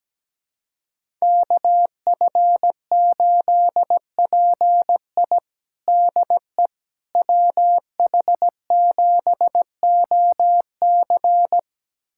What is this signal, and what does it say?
Morse code, heard as a keyed tone, about 17 wpm, 700 Hz. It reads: KF8PI DE WH7OC
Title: USA Callsign DE Callsign - 5 Characters - 17wpm